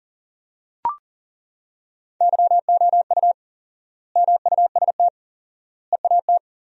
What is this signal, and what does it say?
Morse code, heard as a keyed tone, about 40 wpm, 700 Hz. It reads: YOU MUST EAT
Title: 4x Word-Spacing - Sentences from Top 300 Words - 40wpm